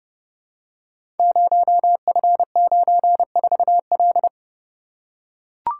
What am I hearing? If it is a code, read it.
Morse code, heard as a keyed tone, about 30 wpm, 700 Hz. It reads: 0F94L
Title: Code Groups - 5 Letter:Numbers - 30wpm